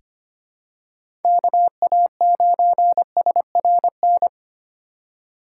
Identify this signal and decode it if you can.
Morse code, heard as a keyed tone, about 25 wpm, 700 Hz. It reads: KA9SRN
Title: USA Call Signs - 6 Characters - 4x Character Spacing - 25wpm